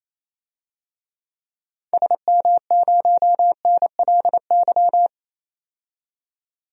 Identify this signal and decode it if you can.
Morse code, heard as a keyed tone, about 28 wpm, 700 Hz. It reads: SM0NLY
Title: DXCC-100 Call Signs - 6 Characters - 28wpm